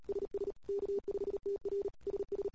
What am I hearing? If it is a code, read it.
Morse code, heard as a keyed tone, about 38 wpm, 400 Hz. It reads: SS K5TR SS
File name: CW / 2008-ss-cw-k5tr-op-n5rz -21:00